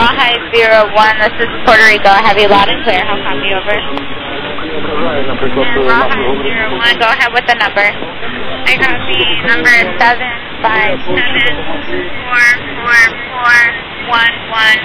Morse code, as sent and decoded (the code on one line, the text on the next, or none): none